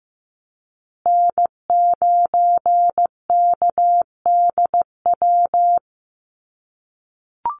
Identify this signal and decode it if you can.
Morse code, heard as a keyed tone, about 15 wpm, 700 Hz. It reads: N9KDW
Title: USA Call Signs Repeated 3x 15wpm